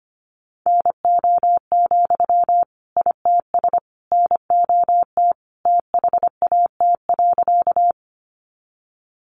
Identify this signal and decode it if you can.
Morse code, heard as a keyed tone, about 25 wpm, 700 Hz. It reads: NO, ITS NOT THAT.
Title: Sentences from Top 600 Words 25wpm